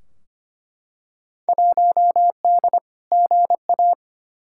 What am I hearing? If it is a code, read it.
Morse code, heard as a keyed tone, about 25 wpm, 700 Hz. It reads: 1D GA
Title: Field Day Exchanges - 25wpm